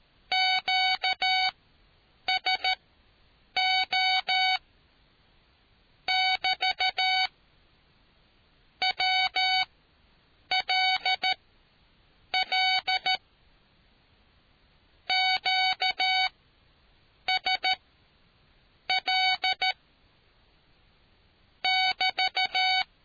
Morse code, heard as a keyed tone, about 13 wpm, spaced as for 7 wpm, 750 Hz. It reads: QSO <BT> WLL QSL <BT>